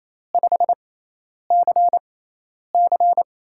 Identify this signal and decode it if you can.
Morse code, heard as a keyed tone, about 28 wpm, spaced as for 10 wpm, 700 Hz. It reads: 5CC
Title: USA Call Signs - 6x Character Spacing - 28wpm